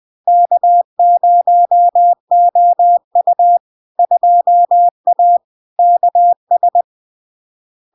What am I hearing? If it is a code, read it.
Morse code, heard as a keyed tone, about 20 wpm, 700 Hz. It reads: K0OU 2A KS
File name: LARK-FD-2022-Log-in-CW-20wpm-700-Hz